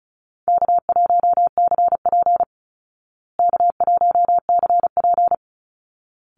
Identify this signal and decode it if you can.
Morse code, heard as a keyed tone, about 34 wpm, 700 Hz. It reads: K1CP K1CP
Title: Easy Call Signs Repeated 2x - 35wpm